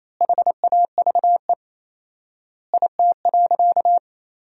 Morse code, heard as a keyed tone, about 28 wpm, 700 Hz. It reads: HAVE IT.
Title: 4x Word-Spacing - Sentences from Top 100 Words - 28wpm